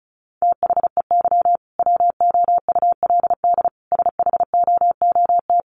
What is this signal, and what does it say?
Morse code, heard as a keyed tone, about 34 wpm, 700 Hz. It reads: THEY WOULD SHOOT
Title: Sentences from Top 1200 Words 35wpm